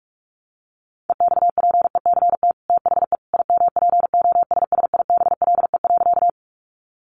Morse code, heard as a keyed tone, about 44 wpm, 700 Hz. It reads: EXPECT THE IMPOSSIBLE.
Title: Sentences from Top 1100 Words 45wpm